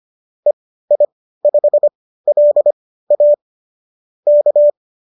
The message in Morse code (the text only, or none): EI5LA K